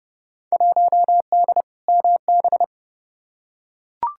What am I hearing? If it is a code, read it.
Morse code, heard as a keyed tone, about 30 wpm, 700 Hz. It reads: 1D MB
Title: Field Day Exchanges Encore - 30wpm